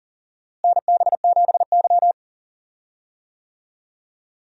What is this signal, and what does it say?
Morse code, heard as a keyed tone, about 40 wpm, 700 Hz. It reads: NB7Y
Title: USA Call Signs - 4 Characters - Repeated 2x - 40wpm